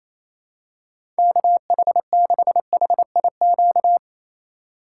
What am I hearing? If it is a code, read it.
Morse code, heard as a keyed tone, about 28 wpm, 700 Hz. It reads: KH6HIQ
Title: USA Call Signs - 6 Characters - 2x Character Spacing - 28wpm